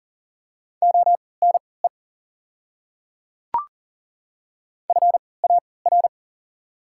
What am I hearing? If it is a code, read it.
Morse code, heard as a keyed tone, about 37 wpm, spaced as for 22 wpm, 700 Hz. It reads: ONE FAR
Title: 3x Character Spacing - Top 300 Words - 40wpm